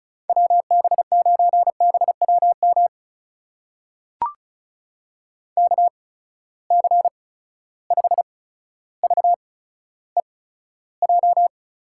Morse code, heard as a keyed tone, about 36 wpm, spaced as for 15 wpm, 700 Hz. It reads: WB9BWM KC5VEJ
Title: USA Call Signs - 6 Characters - 8x Character Spacing - 35wpm